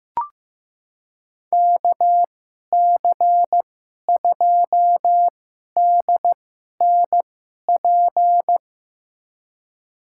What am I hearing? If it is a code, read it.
Morse code, heard as a keyed tone, about 16 wpm, spaced as for 11 wpm, 700 Hz. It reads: KC2DNP
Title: USA Call Signs - 6 Characters - 2x Character Spacing - 15wpm